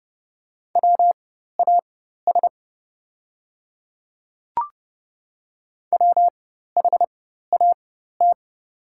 Morse code, heard as a keyed tone, about 30 wpm, spaced as for 14 wpm, 690 Hz. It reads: WAS WHAT
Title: 4x Character Spacing - Top 100 Words - 30wpm